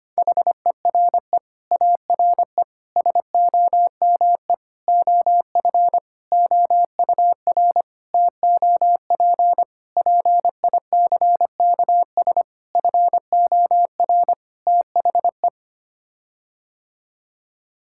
Morse code, heard as a keyed tone, about 25 wpm, 700 Hz. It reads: HERE ARE SOME OF OUR TOP PICKS FOR THE
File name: Sets of 9 Words - 25wpm